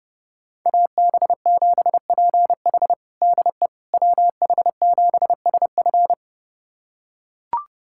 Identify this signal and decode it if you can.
Morse code, heard as a keyed tone, about 30 wpm, 700 Hz. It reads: AB7PH DE WH7SF
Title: USA Callsign DE Callsign - 5 Characters - 30wpm